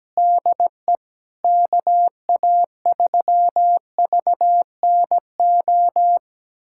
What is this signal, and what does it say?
Morse code, heard as a keyed tone, about 17 wpm, 700 Hz. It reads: DE KA3VNO
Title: USA Callsign DE Callsign - 6 Characters - 17wpm